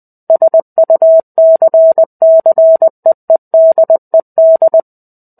Morse code, heard as a keyed tone, about 20 wpm, 650 Hz. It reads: SUCCEEDED
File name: 05-Chapter-05 20 WPM